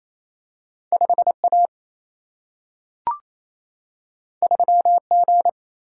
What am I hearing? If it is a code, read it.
Morse code, heard as a keyed tone, about 28 wpm, 700 Hz. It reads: 5A 3G
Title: Single-Letter Single-Number 28wpm